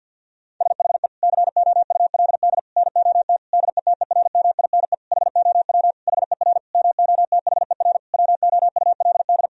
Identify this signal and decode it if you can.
Morse code, heard as a keyed tone, about 50 wpm, 700 Hz. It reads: SHE COULD NOT DETERMINE HOW HER MOTHER WOULD
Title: Sentences from Top 1000 Words - Review Part 4 - 50wpm